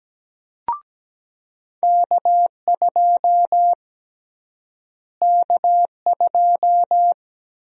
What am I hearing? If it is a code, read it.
Morse code, heard as a keyed tone, about 17 wpm, 700 Hz. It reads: K2 K2
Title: USA Call Sign Prefixes - Repeated 2x - 17wpm